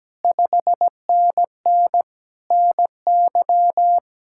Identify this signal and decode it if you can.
Morse code, heard as a keyed tone, about 17 wpm, 700 Hz. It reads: 5NN NY
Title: Speed-Racing Common QSO Phrases - 17wpm